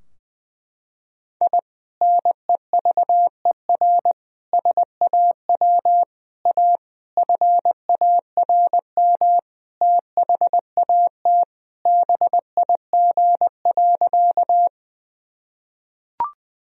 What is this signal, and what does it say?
Morse code, heard as a keyed tone, about 20 wpm, 700 Hz. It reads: I NEVER SAW A FARM THAT BIG.
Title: Sentences from Top 1000 Words - Review Part 2 - 20wpm